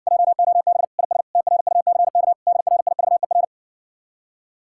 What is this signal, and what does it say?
Morse code, heard as a keyed tone, about 60 wpm, 700 Hz. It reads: JOB IS TRUCK DRIVER